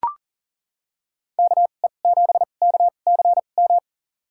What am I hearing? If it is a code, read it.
Morse code, heard as a keyed tone, about 37 wpm, spaced as for 28 wpm, 700 Hz. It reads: KE7KCM
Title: USA Call Signs - 6 Characters - 2x Character Spacing - 40wpm